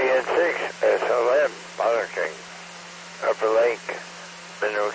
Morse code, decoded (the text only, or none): none